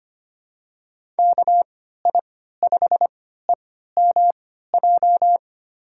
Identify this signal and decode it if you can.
Morse code, heard as a keyed tone, about 25 wpm, spaced as for 14 wpm, 700 Hz. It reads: KI5EMJ
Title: USA Call Signs - 3x Character Spacing - 25wpm